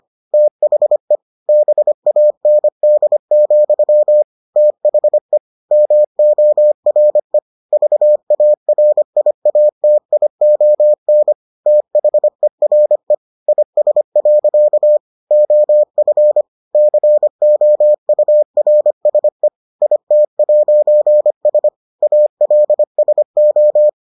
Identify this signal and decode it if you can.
Morse code, heard as a keyed tone, about 25 wpm, 600 Hz. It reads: THE BAND, THE MORE VARIATION THERE IS. OF COURSE IT'S ALSO